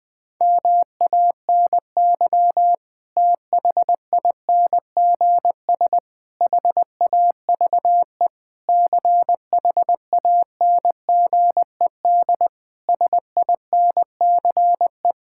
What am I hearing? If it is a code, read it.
Morse code, heard as a keyed tone, about 20 wpm, 700 Hz. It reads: MANY THINGS HAVE CHANGED SINCE
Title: Sentences with 7-Word Phrases - 20wpm